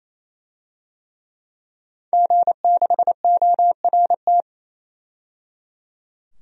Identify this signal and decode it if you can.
Morse code, heard as a keyed tone, about 28 wpm, 700 Hz. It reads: G6ORT
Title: DXCC-100 Call Signs - 5 Characters - 28wpm